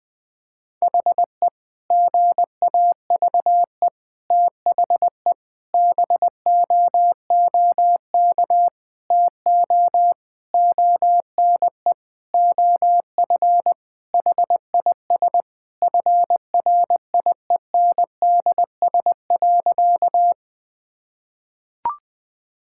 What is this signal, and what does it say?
Morse code, heard as a keyed tone, about 21 wpm, 700 Hz. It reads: HE GAVE THE BOOK TO ONE OF HIS FRIENDS.
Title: Sentences with 4-Word Phrases - 20wpm